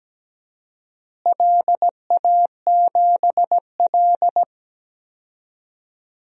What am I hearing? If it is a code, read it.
Morse code, heard as a keyed tone, about 17 wpm, 700 Hz. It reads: LA7L
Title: DXCC-100 Call Signs - 4 Characters - Repeated 2x - 17wpm